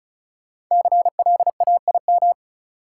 Morse code, heard as a keyed tone, about 34 wpm, 700 Hz. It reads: CLAIM
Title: Sentences from Top 1500 Words - New Word Drill Alt - 35wpm